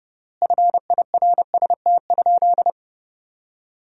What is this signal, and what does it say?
Morse code, heard as a keyed tone, about 30 wpm, 700 Hz. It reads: FIRST?
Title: 3x Word-Spacing - Sentences from Top 100 Words - 30wpm